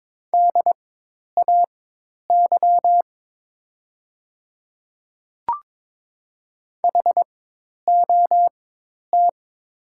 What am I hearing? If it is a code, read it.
Morse code, heard as a keyed tone, about 22 wpm, spaced as for 10 wpm, 700 Hz. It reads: DAY HOT
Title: 4x Character Spacing - Top 200 Words - 22wpm